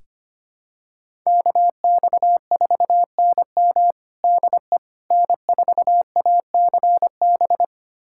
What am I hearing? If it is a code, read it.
Morse code, heard as a keyed tone, about 25 wpm, 700 Hz. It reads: KX4NM DE N4ACB
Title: USA Callsign DE Callsign 25wpm